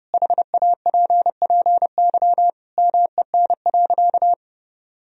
Morse code, heard as a keyed tone, about 30 wpm, 700 Hz. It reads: HAPPY MEN.
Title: Sentences from Top 1000 Words - Review Part 1 - 30wpm